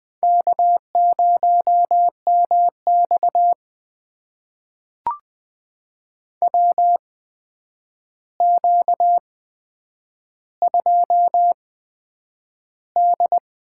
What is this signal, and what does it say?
Morse code, heard as a keyed tone, about 20 wpm, spaced as for 9 wpm, 700 Hz. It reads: K0MX WQ2D
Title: USA Call Signs - 4 Characters - 8x Character Spacing - 20wpm